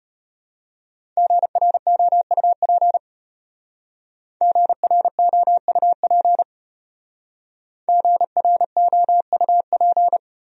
Morse code, heard as a keyed tone, about 32 wpm, 700 Hz. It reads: GROUP GROUP GROUP